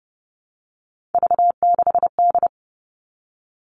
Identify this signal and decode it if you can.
Morse code, heard as a keyed tone, about 30 wpm, 700 Hz. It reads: V6D
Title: All Cut Numbers - 3-Digit 30wpm